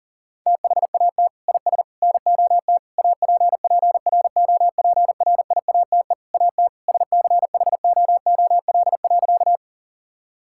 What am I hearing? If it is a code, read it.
Morse code, heard as a keyed tone, about 40 wpm, 700 Hz. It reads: THAT IS NOT APPROPRIATE AT SCHOOL.
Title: Sentences from Top 1000 Words - Review Part 5 - 40wpm